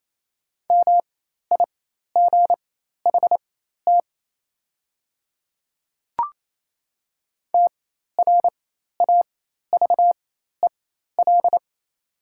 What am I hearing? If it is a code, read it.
Morse code, heard as a keyed tone, about 28 wpm, spaced as for 13 wpm, 700 Hz. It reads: MIGHT TRAVEL
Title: 4x Character Spacing - Top 500 Words - 28wpm